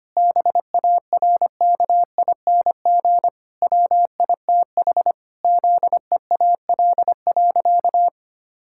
Morse code, heard as a keyed tone, about 25 wpm, 700 Hz. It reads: BARKING WITH ZEAL.
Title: Pangram Challenge v2 - 25wpm